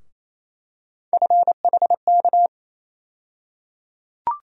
Morse code, heard as a keyed tone, about 27 wpm, 700 Hz. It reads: FHK
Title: USA Call Sign Postfixes - 3 Letters 28wpm